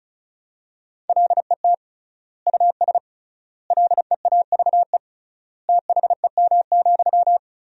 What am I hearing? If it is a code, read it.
Morse code, heard as a keyed tone, about 36 wpm, 700 Hz. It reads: LET US LEAVE THEM,
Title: 3x Word-Spacing - Sentences from Top 400 Words - 35wpm